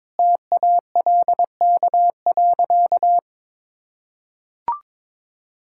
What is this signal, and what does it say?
Morse code, heard as a keyed tone, about 22 wpm, 700 Hz. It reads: TALK.
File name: practice / 22wpm / Sentences with 4-Word Phrases - 22wpm